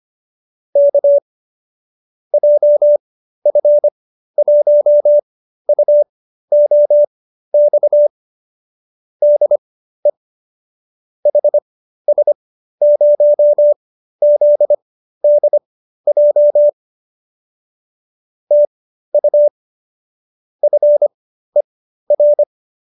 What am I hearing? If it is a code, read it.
Morse code, heard as a keyed tone, about 25 wpm, spaced as for 13 wpm, 590 Hz. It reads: K JF1UOX DE HS0ZDJ TU FER